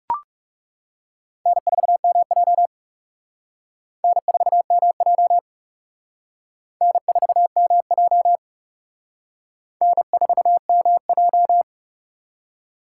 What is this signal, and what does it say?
Morse code, heard as a keyed tone, about 35 wpm, 700 Hz. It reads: N4MJ N4MJ N4MJ N4MJ
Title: Speed Racing - USA Call Signs 30wpm